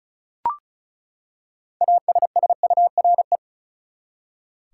Morse code, heard as a keyed tone, about 34 wpm, 700 Hz. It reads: ASSURE
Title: Six Letter Words - 35wpm